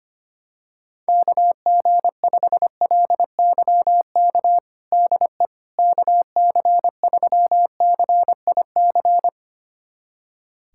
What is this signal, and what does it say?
Morse code, heard as a keyed tone, about 25 wpm, 700 Hz. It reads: KG5LYK DE KC3CIC
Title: USA Callsign DE Callsign - 6 Characters - 25wpm